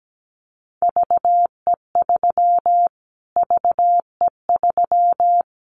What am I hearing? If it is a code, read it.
Morse code, heard as a keyed tone, about 17 wpm, 700 Hz. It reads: VE3 VE3
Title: Callsign Prefixes for Top 25 DXCC - Repeated 3x - 17wpm